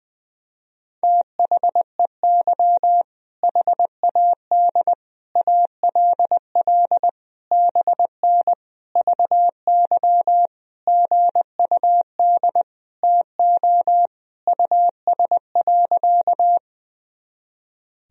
Morse code, heard as a keyed tone, about 21 wpm, 700 Hz. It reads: THEY HAD ALL BN VY GUD TO US.
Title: Sentences from Top 200 Words and Abbreviations 20wpm